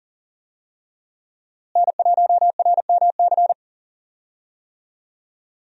Morse code, heard as a keyed tone, about 40 wpm, 700 Hz. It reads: N1RMC
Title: USA Call Signs - 5 Characters - Repeated 3x - 40wpm